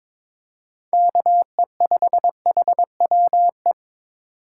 Morse code, heard as a keyed tone, about 21 wpm, 700 Hz. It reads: KE5HWE
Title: USA Call Signs - 6 Characters - 4x Character Spacing - 22wpm